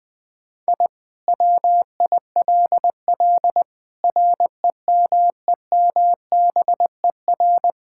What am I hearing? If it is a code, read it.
Morse code, heard as a keyed tone, about 20 wpm, 700 Hz. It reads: I WILL REMEMBER
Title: Sentences with 5-Word Phrases - 20wpm